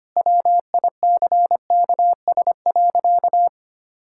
Morse code, heard as a keyed tone, about 25 wpm, 700 Hz. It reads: WICKS.
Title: Pangram Challenge v2 - 25wpm